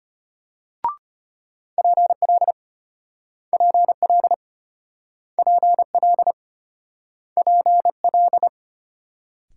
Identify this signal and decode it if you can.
Morse code, heard as a keyed tone, about 34 wpm, 700 Hz. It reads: PL PL PL PL